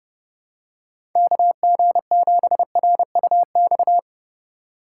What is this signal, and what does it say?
Morse code, heard as a keyed tone, about 30 wpm, 700 Hz. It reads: KG7RUX
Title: USA Call Signs - 6 Characters - 3x Character Spacing - 30wpm